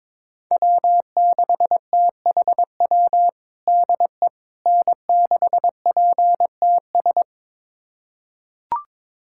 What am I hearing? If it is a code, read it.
Morse code, heard as a keyed tone, about 22 wpm, 700 Hz. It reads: W6THW DE N6PTS
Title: USA Callsign DE Callsign - 5 Characters - 22wpm